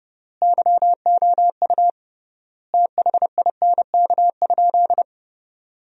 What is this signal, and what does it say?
Morse code, heard as a keyed tone, about 32 wpm, 700 Hz. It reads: YOU THINK?
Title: 3x Word-Spacing - Sentences from Top 200 Words - 30wpm